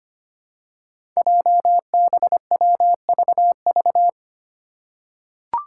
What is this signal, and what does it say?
Morse code, heard as a keyed tone, about 25 wpm, 700 Hz. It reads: JBWVV
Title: Five Challenging Letter-Number Combinations 25wpm